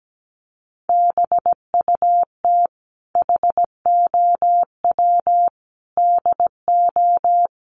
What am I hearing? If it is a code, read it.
Morse code, heard as a keyed tone, about 17 wpm, 700 Hz. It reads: BUT HOW DO
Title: Sentences from Top 900 Words 17wpm